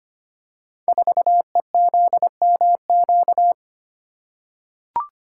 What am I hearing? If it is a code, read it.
Morse code, heard as a keyed tone, about 25 wpm, 700 Hz. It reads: 4EZMQ
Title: Code Groups - 5 Letter:Numbers - 25wpm